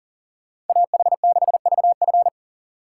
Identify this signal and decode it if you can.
Morse code, heard as a keyed tone, about 49 wpm, 700 Hz. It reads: AH6VF
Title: Speed Racing - USA Call Signs 40wpm